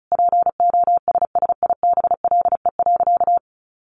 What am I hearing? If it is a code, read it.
Morse code, heard as a keyed tone, about 34 wpm, 700 Hz. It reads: POSSIBLE.
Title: Sentences from Top 500 Words 35wpm